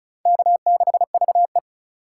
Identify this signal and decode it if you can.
Morse code, heard as a keyed tone, about 42 wpm, 700 Hz. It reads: K6VE
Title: Speed Racing - USA 1x2 Call Signs 35wpm